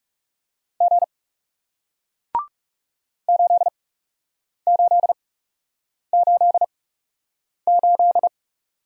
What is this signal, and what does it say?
Morse code, heard as a keyed tone, about 41 wpm, spaced as for 8 wpm, 700 Hz. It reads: G 8888